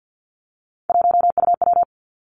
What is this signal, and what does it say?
Morse code, heard as a keyed tone, about 50 wpm, 700 Hz. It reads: 1UW
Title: Three Challenging Letter-Number Combinations 50wpm